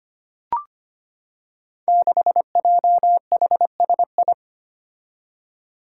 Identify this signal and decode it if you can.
Morse code, heard as a keyed tone, about 25 wpm, 700 Hz. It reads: 6JHSI
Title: Five Challenging Letter-Number Combinations 25wpm